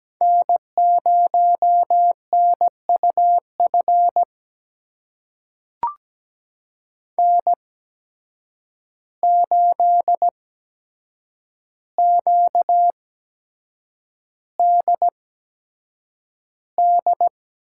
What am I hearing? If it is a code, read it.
Morse code, heard as a keyed tone, about 17 wpm, spaced as for 7 wpm, 700 Hz. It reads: N0NUF N8QDD
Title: USA Call Signs - 5 Characters - 8x Character Spacing - 17wpm